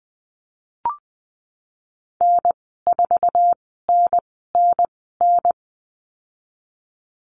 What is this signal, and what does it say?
Morse code, heard as a keyed tone, about 20 wpm, spaced as for 14 wpm, 700 Hz. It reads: N4NNN
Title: 069 Lesson 12g - Letter d - Call Signs 20wpm